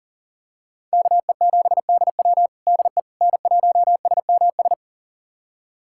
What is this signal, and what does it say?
Morse code, heard as a keyed tone, about 40 wpm, 690 Hz. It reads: KE7DW DE N1SMS
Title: USA Callsign DE Callsign - 5 Characters - 40wpm